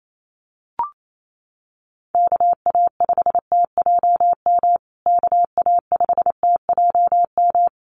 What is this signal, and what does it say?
Morse code, heard as a keyed tone, about 28 wpm, 700 Hz. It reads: KA5TJM KA5TJM
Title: USA Call Signs Repeated 3x 28wpm